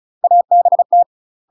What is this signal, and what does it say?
Morse code, heard as a keyed tone, about 36 wpm, 700 Hz. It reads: ABT
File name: UK MASTER QSOs TRAINING / 35WPM / UK MASTER QSOs 0029